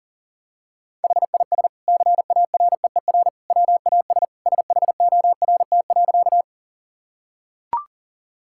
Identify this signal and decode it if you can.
Morse code, heard as a keyed tone, about 40 wpm, 700 Hz. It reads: HIS CAREER WAS SHORT.
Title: Sentences from Top 1000 Words - Review Part 6 - 40wpm